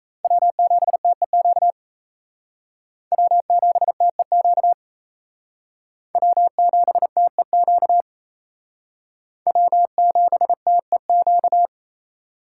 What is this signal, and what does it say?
Morse code, heard as a keyed tone, about 35 wpm, 700 Hz. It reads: W7TEQ W7TEQ W7TEQ W7TEQ
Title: Speed Racing - USA 1x3 Call Signs 28wpm